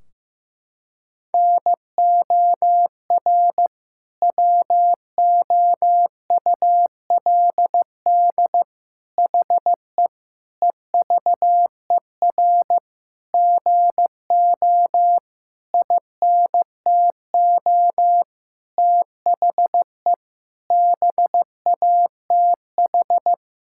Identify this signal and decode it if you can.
Morse code, heard as a keyed tone, about 15 wpm, 700 Hz. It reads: NOR WOULD HE EVER GO INTO THE BATH
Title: Sentences from Top 1600 Words 15wpm